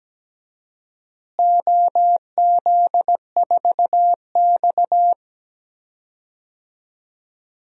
Morse code, heard as a keyed tone, about 17 wpm, 700 Hz. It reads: OZ4X
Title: DXCC-100 Call Signs - 4 Characters - Repeated 3x - 17wpm